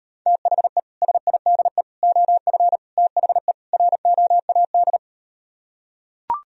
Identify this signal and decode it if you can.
Morse code, heard as a keyed tone, about 39 wpm, 700 Hz. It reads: THE SIDE OF THE ROAD